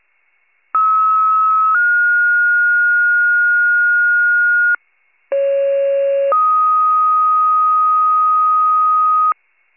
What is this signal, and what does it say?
Two-tone page, heard as a one-second tone, then a three-second tone, about 1290 then 1430 Hz, about 550 then 1250 Hz. It is heard from 0.7 to 9.3 s.
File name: pager / 28 FIRE / wcalert